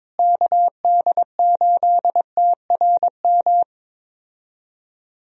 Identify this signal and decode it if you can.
Morse code, heard as a keyed tone, about 22 wpm, 700 Hz. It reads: KD8TRM
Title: USA Call Signs - 6 Characters - Repeated 2x - 22wpm